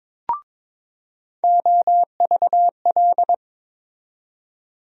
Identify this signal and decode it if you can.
Morse code, heard as a keyed tone, about 22 wpm, 700 Hz. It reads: OVL
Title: USA Call Sign Postfixes - 3 Letters 22wpm